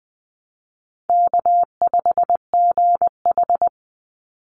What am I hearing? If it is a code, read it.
Morse code, heard as a keyed tone, about 20 wpm, 700 Hz. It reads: K5GH
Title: Easy Call Signs - 20wpm